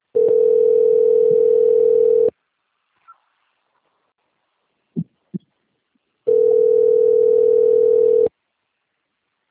Ringback tone (North America), 440 + 480 Hz.
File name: out- - - - - .53298